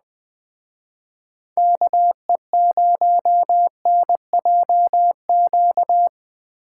Morse code, heard as a keyed tone, about 20 wpm, 700 Hz. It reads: KE0NJQ
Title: Speed Racing - USA 2x3 Call Signs 17wpm